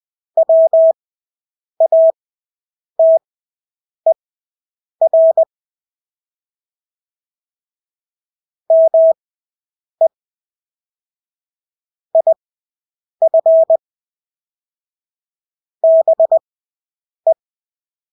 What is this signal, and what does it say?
Morse code, heard as a keyed tone, about 20 wpm, spaced as for 8 wpm, 650 Hz. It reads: WATER ME IF BE
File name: Engelske ord 20/8 wpm